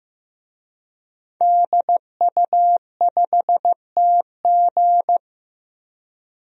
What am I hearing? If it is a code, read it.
Morse code, heard as a keyed tone, about 15 wpm, 700 Hz. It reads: DU5TG
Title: DXCC-100 Call Signs - 5 Characters - 15wpm